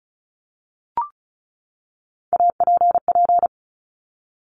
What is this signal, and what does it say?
Morse code, heard as a keyed tone, about 36 wpm, 700 Hz. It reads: APP
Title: Top Three Letter Combinations in Words 35wpm